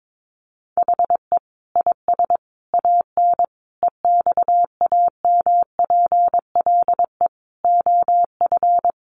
Morse code, heard as a keyed tone, about 22 wpm, 700 Hz. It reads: HE IS AN EXAMPLE OF
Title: Sentences from Top 1500 Words 22wpm